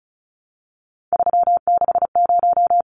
Morse code, heard as a keyed tone, about 34 wpm, 700 Hz. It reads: 360
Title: Numbers - 3-Digit 35wpm